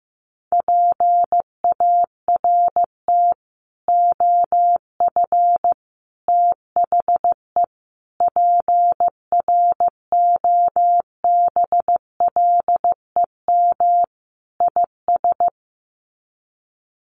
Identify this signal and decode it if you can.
Morse code, heard as a keyed tone, about 15 wpm, 700 Hz. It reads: PART OF THE PROBLEM IS
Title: Sets of 5 Words Encore2 15wpm